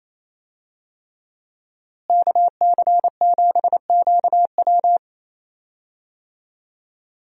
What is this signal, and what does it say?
Morse code, heard as a keyed tone, about 28 wpm, 700 Hz. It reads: KC7QW
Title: USA Call Signs - 5 Characters - Repeated 3x - 28wpm